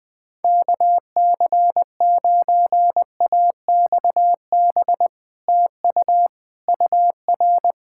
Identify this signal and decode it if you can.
Morse code, heard as a keyed tone, about 20 wpm, 700 Hz. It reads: KC9AXB TU UR
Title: Speed-Racing Common QSO Phrases - 20wpm